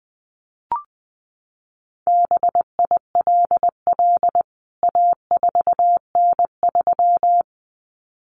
Morse code, heard as a keyed tone, about 20 wpm, 700 Hz. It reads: BILL A4N3
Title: CWT Exchange 20wpm